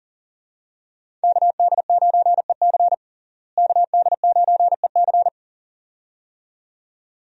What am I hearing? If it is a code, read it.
Morse code, heard as a keyed tone, about 37 wpm, 700 Hz. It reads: KD9EC KD9EC
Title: USA Call Signs - 5 Characters - Repeated 2x - 40wpm